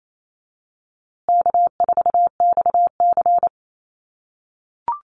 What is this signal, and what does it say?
Morse code, heard as a keyed tone, about 28 wpm, 700 Hz. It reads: K4XC
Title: USA Call Signs Repeated 3x 28wpm